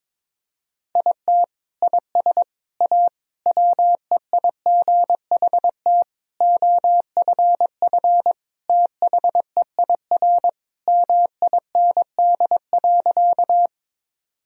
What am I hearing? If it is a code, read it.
Morse code, heard as a keyed tone, about 21 wpm, 700 Hz. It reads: IT IS A WEIGHT OFF THEIR MIND.
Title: Sentences from Top 1000 Words - Review Part 3 - 22wpm